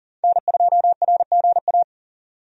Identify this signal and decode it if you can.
Morse code, heard as a keyed tone, about 40 wpm, 700 Hz. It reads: N2RGA
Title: USA Call Signs - 5 Characters - 40wpm